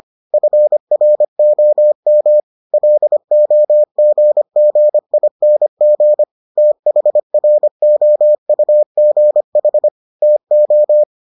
Morse code, heard as a keyed tone, about 25 wpm, 600 Hz. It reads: FROM LOGGING THROUGH TO